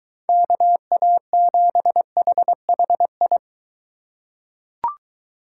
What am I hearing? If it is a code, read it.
Morse code, heard as a keyed tone, about 21 wpm, 700 Hz. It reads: KA7HHI